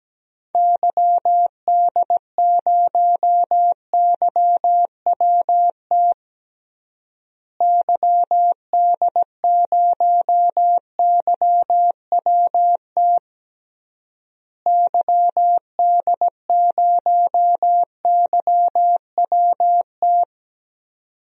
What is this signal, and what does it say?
Morse code, heard as a keyed tone, about 17 wpm, 700 Hz. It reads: YD0YWT YD0YWT YD0YWT
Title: DXCC-100 Call Signs - 6 Characters - Repeated 3x - 17wpm